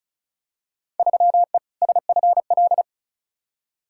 Morse code, heard as a keyed tone, about 34 wpm, 700 Hz. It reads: 3E SFL
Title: Field Day Exchanges Encore - 35wpm